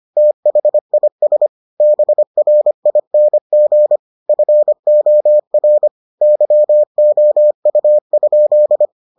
Morse code, heard as a keyed tone, about 25 wpm, 600 Hz. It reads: THIS BRING FOR YOU?